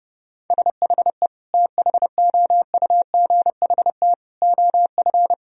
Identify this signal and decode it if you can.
Morse code, heard as a keyed tone, about 30 wpm, 700 Hz. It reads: SHE THOUGHT OF
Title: Sentences from Top 1000 Words - Review Part 1 - 30wpm